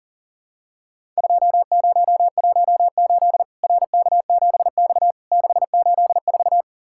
Morse code, heard as a keyed tone, about 40 wpm, 700 Hz. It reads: 2018 RK7X 684
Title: Worked All Europe DX Contest - QTC Exchange - 40wpm